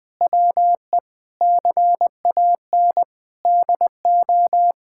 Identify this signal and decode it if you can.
Morse code, heard as a keyed tone, about 19 wpm, 700 Hz. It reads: WE CAN DO
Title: Sentences from Top 1000 Words - Review Part 6 - 20wpm